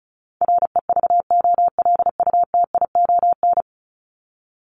Morse code, heard as a keyed tone, about 34 wpm, 700 Hz. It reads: REVOLUTION
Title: Top 1751-2000 Words 35wpm